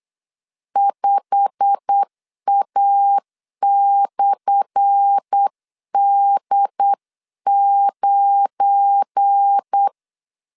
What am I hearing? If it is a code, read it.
Morse code, heard as a keyed tone, about 8 wpm, 800 Hz. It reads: 5A/D9
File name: LEZIONI / 10 SIGNS / 10A SLASH 8wpm 3452